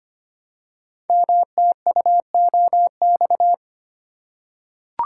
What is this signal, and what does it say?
Morse code, heard as a keyed tone, about 25 wpm, 700 Hz. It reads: MTUOX
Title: Code Groups - 5 Letters - 25wpm